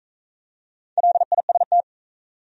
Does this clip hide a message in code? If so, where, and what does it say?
Yes: Morse code from 1.0 to 1.8 s, heard as a keyed tone, about 44 wpm, 700 Hz. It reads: LIST